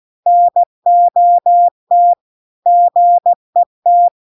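Morse code, heard as a keyed tone, about 16 wpm, 700 Hz. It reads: NOT GET
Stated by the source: Morse practice text at 16 WPM.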